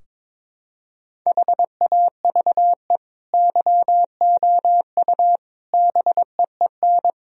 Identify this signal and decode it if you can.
Morse code, heard as a keyed tone, about 22 wpm, 700 Hz. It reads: HAVE YOU BEEN
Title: Sentences from Top 1000 Words - Review Part 4 - 22wpm